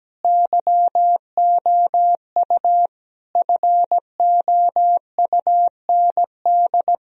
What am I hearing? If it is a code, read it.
Morse code, heard as a keyed tone, about 17 wpm, 700 Hz. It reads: YOU FOUND
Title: Sentences from Top 1000 Words - Review Part 4 - 17wpm